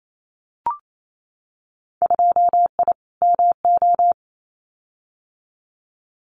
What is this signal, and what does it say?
Morse code, heard as a keyed tone, about 28 wpm, 700 Hz. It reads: 2I MO
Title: Winter Field Day Exchanges 28wpm